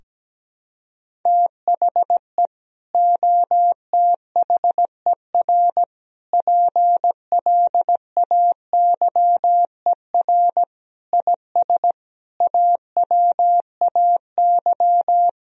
Morse code, heard as a keyed tone, about 17 wpm, 700 Hz. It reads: THE OTHER PLAYER IS AWAY
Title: Sentences from Top 1000 Words - Review Part 4 - 17wpm